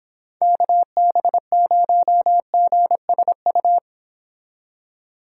Morse code, heard as a keyed tone, about 29 wpm, 700 Hz. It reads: KB0GSU